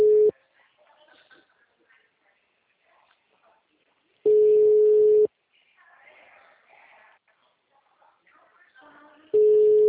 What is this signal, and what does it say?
Ringback tone (Europe, 425 Hz).